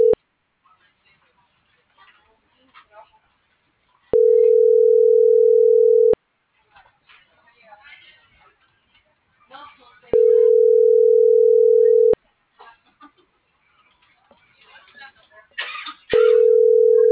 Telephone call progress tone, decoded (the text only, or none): ringback tone